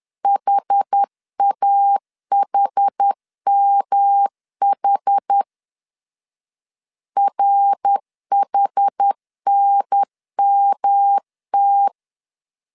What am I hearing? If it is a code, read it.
Morse code, heard as a keyed tone, about 10 wpm, 800 Hz. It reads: HAHMH RHNMT
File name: CW / DET / 1 3 1500C S B 10wpm 3335